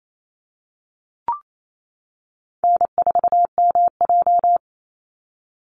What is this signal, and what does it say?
Morse code, heard as a keyed tone, about 28 wpm, 700 Hz. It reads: N4MJ
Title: NA SOTA Chaser Callsigns 28wpm